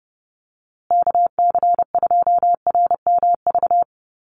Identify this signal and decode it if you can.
Morse code, heard as a keyed tone, about 30 wpm, 700 Hz. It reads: KC2RMV
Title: USA Call Signs Repeated 3x 30wpm